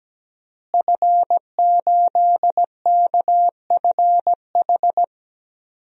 Morse code, heard as a keyed tone, about 17 wpm, 700 Hz. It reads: F8KFH
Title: DXCC-100 Call Signs - 5 Characters - Repeated 2x - 17wpm